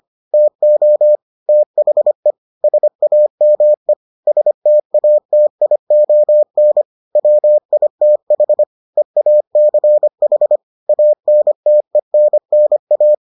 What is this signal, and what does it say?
Morse code, heard as a keyed tone, about 25 wpm, 600 Hz. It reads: TO THE SAME STATION WITH EACH ANTENNA